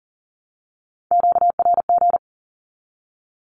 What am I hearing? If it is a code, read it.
Morse code, heard as a keyed tone, about 40 wpm, 700 Hz. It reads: QRG
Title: Q-Codes 40wpm